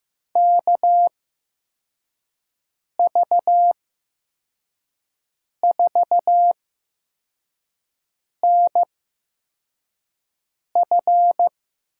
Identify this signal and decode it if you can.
Morse code, heard as a keyed tone, about 15 wpm, spaced as for 4 wpm, 700 Hz. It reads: KV4NF
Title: USA Call Signs - 8x Character Spacing - 15wpm